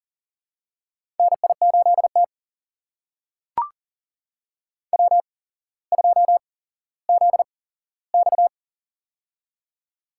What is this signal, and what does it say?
Morse code, heard as a keyed tone, about 40 wpm, spaced as for 17 wpm, 690 Hz. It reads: NI8T W2ZX
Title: USA Call Signs - 4 Characters - 8x Character Spacing - 40wpm